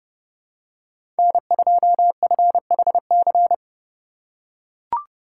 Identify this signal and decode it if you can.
Morse code, heard as a keyed tone, about 30 wpm, 700 Hz. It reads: N2FHC
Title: USA Call Signs - 5 Characters - 3x Character Spacing - 30wpm